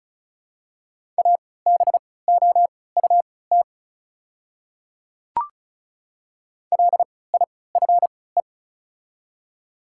Morse code, heard as a keyed tone, about 36 wpm, spaced as for 20 wpm, 700 Hz. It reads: ABOUT LIFE
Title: 3x Character Spacing - Top 500 Words - 35wpm